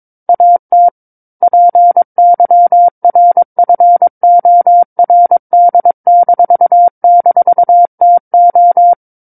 Morse code, heard as a keyed tone, about 22 wpm, 700 Hz. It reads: AT PYRFORD--TO
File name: Book One / 13-Chapter 13 - 22 WPM